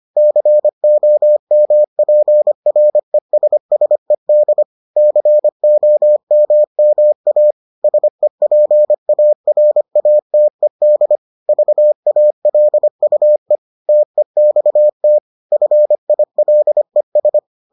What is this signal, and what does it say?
Morse code, heard as a keyed tone, about 25 wpm, 600 Hz. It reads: COMPRESSED COMMA SEPARATED VALUE TEXT FILES